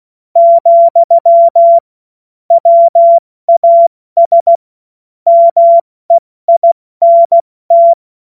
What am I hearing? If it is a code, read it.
Morse code, heard as a keyed tone, about 16 wpm, spaced as for 14 wpm, 680 Hz. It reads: , WAS MEINT
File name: Morsekurs / Witze / Witz 21.November 2023 70 F80